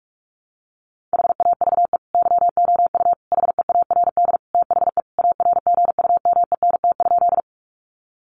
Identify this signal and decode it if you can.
Morse code, heard as a keyed tone, about 44 wpm, 700 Hz. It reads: HAVE YOU HEARD THE ARGUMENT?
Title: Sentences from Top 1400 Words 45wpm